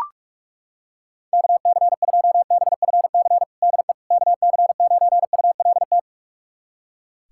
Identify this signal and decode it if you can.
Morse code, heard as a keyed tone, about 42 wpm, 700 Hz. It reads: KC2BFC DE KC9ULT
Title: USA Callsign DE Callsign - 6 Characters - 45wpm